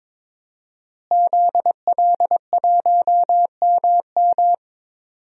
Morse code, heard as a keyed tone, about 22 wpm, 700 Hz. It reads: ZL1MM
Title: DXCC-100 Call Signs - 5 Characters - Repeated 3x - 22wpm